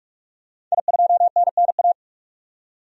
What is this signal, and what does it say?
Morse code, heard as a keyed tone, about 46 wpm, 700 Hz. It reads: I2NNA
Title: DXCC-100 Call Signs - 5 Characters - 45wpm